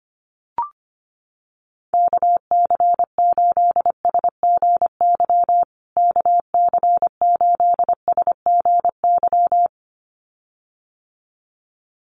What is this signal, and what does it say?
Morse code, heard as a keyed tone, about 25 wpm, 700 Hz. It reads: KC8SGY KC8SGY
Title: USA Call Signs Repeated 2x 25wpm